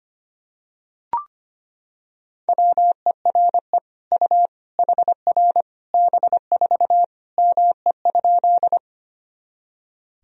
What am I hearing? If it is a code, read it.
Morse code, heard as a keyed tone, about 25 wpm, 700 Hz. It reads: WERE U HR B4 ME?
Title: Sentences from Top 200 Words and Abbreviations 25wpm